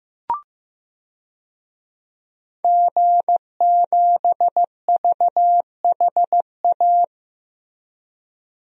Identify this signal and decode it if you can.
Morse code, heard as a keyed tone, about 15 wpm, 700 Hz. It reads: G7VHA
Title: DXCC-100 Call Signs - 5 Characters - 15wpm